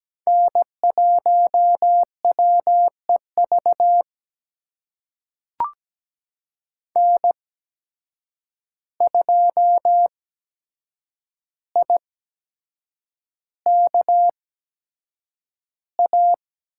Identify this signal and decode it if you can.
Morse code, heard as a keyed tone, about 17 wpm, spaced as for 7 wpm, 700 Hz. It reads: N1WEV N2IKA
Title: USA Call Signs - 5 Characters - 8x Character Spacing - 17wpm